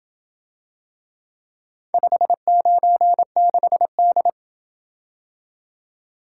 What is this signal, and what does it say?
Morse code, heard as a keyed tone, about 27 wpm, 700 Hz. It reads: 596D